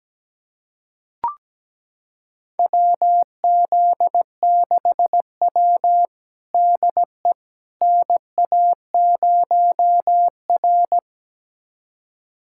Morse code, heard as a keyed tone, about 17 wpm, 700 Hz. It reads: WZ6W DE NA0R
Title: USA Callsign DE Callsign - 4 Characters - 17wpm